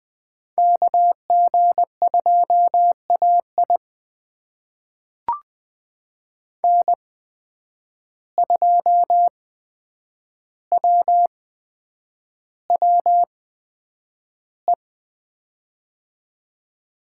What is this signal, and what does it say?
Morse code, heard as a keyed tone, about 20 wpm, spaced as for 9 wpm, 700 Hz. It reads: KG2AI N2WWE
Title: USA Call Signs - 5 Characters - 8x Character Spacing - 20wpm